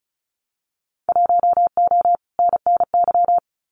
Morse code, heard as a keyed tone, about 34 wpm, 700 Hz. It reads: 1O NNY
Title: Winter Field Day Exchanges 35wpm